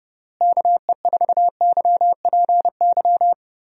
Morse code, heard as a keyed tone, about 30 wpm, 700 Hz. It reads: KE4YPY
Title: USA Call Signs - 6 Characters - Repeated 3x - 30wpm